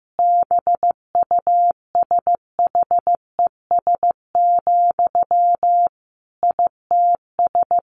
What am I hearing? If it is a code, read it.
Morse code, heard as a keyed tone, about 15 wpm, 700 Hz. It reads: BUSHES, ITS
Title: A Sound Of Thunder 15wpm